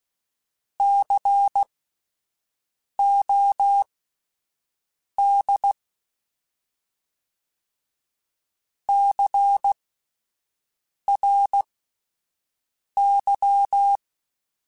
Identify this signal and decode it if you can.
Morse code, heard as a keyed tone, about 16 wpm, spaced as for 5 wpm, 800 Hz. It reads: COD CRY
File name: Spaced Alphas / Spaced Alphas mp3s / Cs sp @10